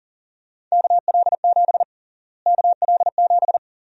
Morse code, heard as a keyed tone, about 40 wpm, 700 Hz. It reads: KL7 KL7
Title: USA Call Sign Prefixes - Repeated 2x - 40wpm